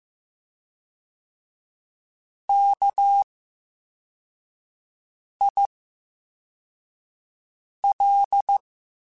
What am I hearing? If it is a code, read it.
Morse code, heard as a keyed tone, about 15 wpm, 800 Hz. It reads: K I L
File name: Special INGs-mp3s / Special INGs-mp3s / VerySlowToFaster Ing @13